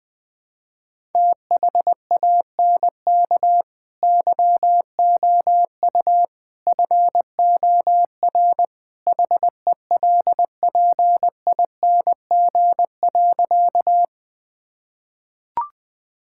Morse code, heard as a keyed tone, about 19 wpm, 700 Hz. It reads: THANK YOU FOR HELPING.
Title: Sentences with 3-Word Phrases - Spoken Beginning - 20wpm